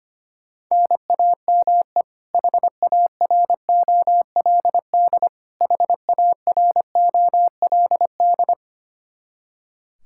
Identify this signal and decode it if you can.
Morse code, heard as a keyed tone, about 25 wpm, 700 Hz. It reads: NAME HAROLD HAROLD
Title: Practice QSO - Exchange 1 - Beginning - 25wpm